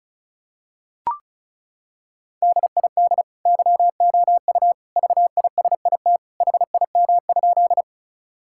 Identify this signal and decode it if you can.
Morse code, heard as a keyed tone, about 36 wpm, 700 Hz. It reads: DID YOU VISIT HIM?
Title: Sentences from Top 1000 Words - Review Part 6 - 35wpm